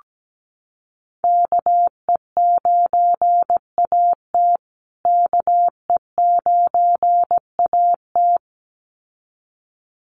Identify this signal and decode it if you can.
Morse code, heard as a keyed tone, about 17 wpm, 700 Hz. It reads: KE9AT KE9AT
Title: USA Call Signs Repeated 2x 17wpm